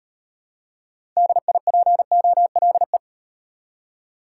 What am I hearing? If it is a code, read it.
Morse code, heard as a keyed tone, about 36 wpm, 700 Hz. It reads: DIPOLE